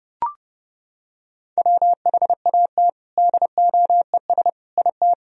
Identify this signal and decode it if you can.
Morse code, heard as a keyed tone, about 30 wpm, 700 Hz. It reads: WHAT DOES IT
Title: Sentences from Top 1000 Words - Review Part 2 - 30wpm